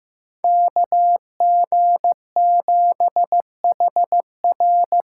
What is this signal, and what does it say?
Morse code, heard as a keyed tone, about 15 wpm, 700 Hz. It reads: KG7HR
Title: USA Call Signs - 6 Characters - 6x Character Spacing - 15wpm